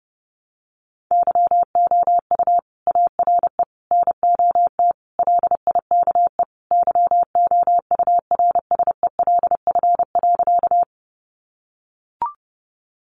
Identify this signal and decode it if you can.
Morse code, heard as a keyed tone, about 30 wpm, 700 Hz. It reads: YOU ARE NOT LIKE YOURSELF.
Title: Sentences from Top 600 Words 30wpm